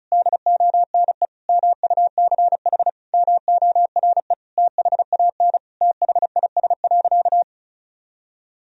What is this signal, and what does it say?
Morse code, heard as a keyed tone, about 34 wpm, 700 Hz. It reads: DONE MUCH MORE THAN THIS.
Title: Sentences from Top 1000 Words - Review Part 1 - 35wpm